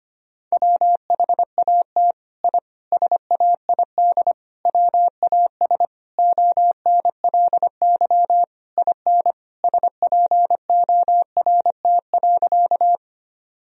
Morse code, heard as a keyed tone, about 25 wpm, 700 Hz. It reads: WHAT I SAID WAS ONLY IN SPORT.
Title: Sentences from Top 1000 Words - Review Part 4 - 25wpm